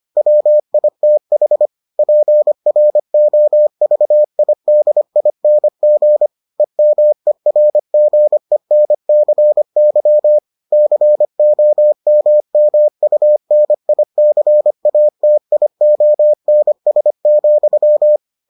Morse code, heard as a keyed tone, about 25 wpm, 600 Hz. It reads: WITH PROVIDING EMERGENCY COMMUNICATIONS,